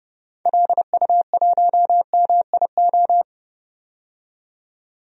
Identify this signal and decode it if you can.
Morse code, heard as a keyed tone, about 30 wpm, 700 Hz. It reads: LU1MIO
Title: DXCC-100 Call Signs - 6 Characters - 30wpm